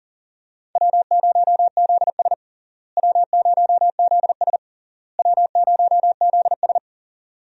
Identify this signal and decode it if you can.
Morse code, heard as a keyed tone, about 40 wpm, 700 Hz. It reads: W0ZS W0ZS W0ZS
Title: USA Call Signs - 4 Characters - Repeated 3x - 40wpm